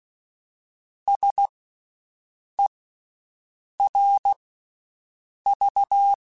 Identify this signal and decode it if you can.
Morse code, heard as a keyed tone, about 16 wpm, 800 Hz. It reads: S E R V
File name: Special INGs-mp3s / Special INGs-mp3s / Slow ing @15